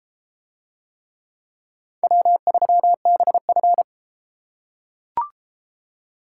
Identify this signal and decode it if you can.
Morse code, heard as a keyed tone, about 32 wpm, 700 Hz. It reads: W3BF